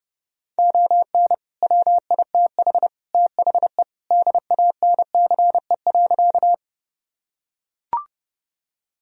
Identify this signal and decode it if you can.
Morse code, heard as a keyed tone, about 32 wpm, 700 Hz. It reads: ON WITH THE DANCE.
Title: Sentences from Top 1000 Words - Review Part 4 - 30wpm